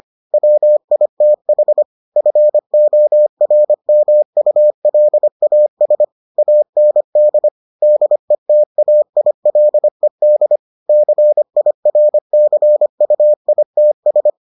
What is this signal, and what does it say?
Morse code, heard as a keyed tone, about 25 wpm, 600 Hz. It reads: WITH FORMULAS AND DETAILED CIRCUITS